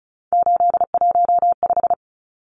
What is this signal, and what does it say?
Morse code, heard as a keyed tone, about 34 wpm, 700 Hz. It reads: 815
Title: Numbers - 3-Digit 35wpm